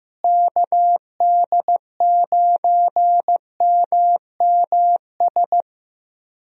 Morse code, heard as a keyed tone, about 15 wpm, 700 Hz. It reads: KD9MMS
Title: USA Call Signs - 6 Characters - Repeated 3x - 15wpm